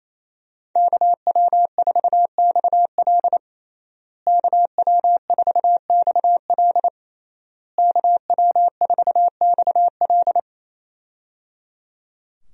Morse code, heard as a keyed tone, about 28 wpm, 700 Hz. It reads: KW4XL KW4XL KW4XL
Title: USA Call Signs - 5 Characters - Repeated 3x - 28wpm